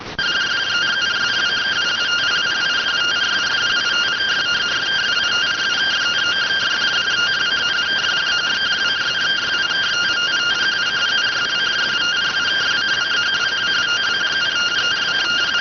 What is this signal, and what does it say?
Two-tone teleprinter signal, about 170 Hz shift.